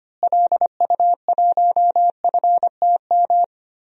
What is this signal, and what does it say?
Morse code, heard as a keyed tone, about 25 wpm, 700 Hz. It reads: LU1FTM
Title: DXCC-100 Call Signs - 6 Characters - Repeated 3x - 25wpm